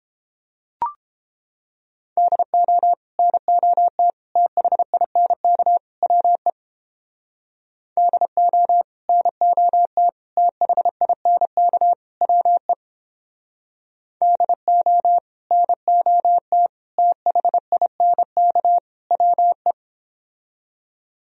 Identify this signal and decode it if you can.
Morse code, heard as a keyed tone, about 31 wpm, 700 Hz. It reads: DO NOT THINK WE DO NOT THINK WE DO NOT THINK WE